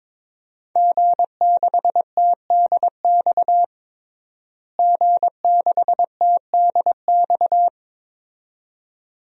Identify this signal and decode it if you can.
Morse code, heard as a keyed tone, about 22 wpm, 700 Hz. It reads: G6TDX G6TDX
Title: DXCC-100 Call Signs - 5 Characters - Repeated 2x - 22wpm